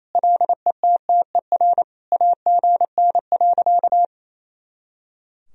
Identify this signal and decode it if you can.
Morse code, heard as a keyed tone, about 28 wpm, 700 Hz. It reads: LETTER AGN.
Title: Sentences from Top 300 Words and Abbreviations 28wpm